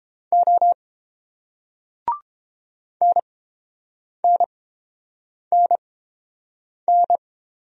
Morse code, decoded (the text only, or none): O NNNN